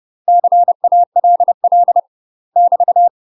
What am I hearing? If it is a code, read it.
Morse code, heard as a keyed tone, about 30 wpm, 700 Hz. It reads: CALL <BT>
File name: UK MASTER QSOs TRAINING / 30WPM / UK MASTER QSOs 0022